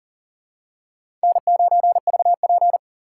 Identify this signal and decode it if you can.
Morse code, heard as a keyed tone, about 37 wpm, 700 Hz. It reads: N9VP
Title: USA Call Signs - 3x Character Spacing - 40wpm